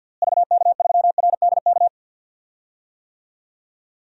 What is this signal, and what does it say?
Morse code, heard as a keyed tone, about 48 wpm, 700 Hz. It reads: VK3RDK
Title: DXCC-100 Call Signs - 6 Characters - Repeated 2x - 50wpm